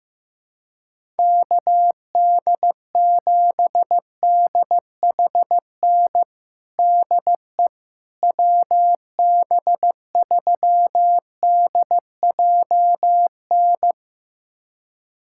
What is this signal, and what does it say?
Morse code, heard as a keyed tone, about 15 wpm, 700 Hz. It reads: KD7DHN DE WB3DJN
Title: USA Callsign DE Callsign 15wpm